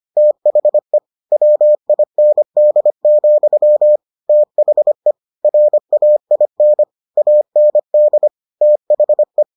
Morse code, heard as a keyed tone, about 25 wpm, 600 Hz. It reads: THE WIND, THE RAIN AND THE